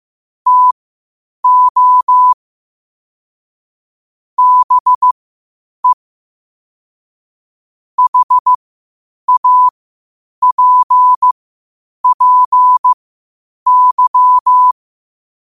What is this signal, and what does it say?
Morse code, heard as a keyed tone, about 15 wpm, spaced as for 8 wpm, 1000 Hz. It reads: TO BE HAPPY
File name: qotdfiles / 8wpm